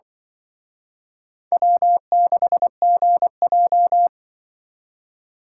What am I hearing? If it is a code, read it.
Morse code, heard as a keyed tone, about 23 wpm, 700 Hz. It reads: W6GJ
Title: Speed Racing - USA 1x2 Call Signs 20wpm